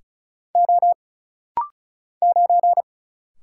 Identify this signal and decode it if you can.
Morse code, heard as a keyed tone, about 36 wpm, 700 Hz. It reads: O 9
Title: Single Letter-Number - Rapid-Fire 35wpm